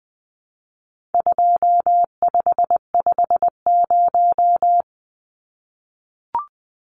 Morse code, heard as a keyed tone, about 20 wpm, 700 Hz. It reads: 2550
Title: Numbers - 4-Digit 20wpm